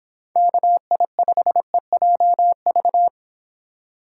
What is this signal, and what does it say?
Morse code, heard as a keyed tone, about 24 wpm, 700 Hz. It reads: KI5EJV